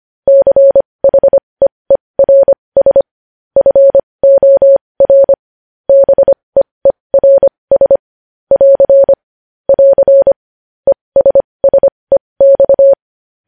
Morse code, heard as a keyed tone, about 25 wpm, spaced as for 20 wpm, 550 Hz. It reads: CHEERS FOR BEERS <AR> <AR> ESSEX
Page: news-headlines-in-morse-code-at-20-wpm